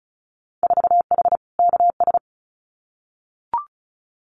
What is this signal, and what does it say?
Morse code, heard as a keyed tone, about 34 wpm, 700 Hz. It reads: 4H KS
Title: Winter Field Day Exchanges Encore 35wpm